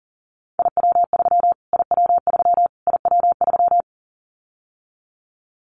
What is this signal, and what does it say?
Morse code, heard as a keyed tone, about 40 wpm, 700 Hz. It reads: IW3 IW3 IW3
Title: Callsign Prefixes for Top 25 DXCC - Repeated 3x - 40wpm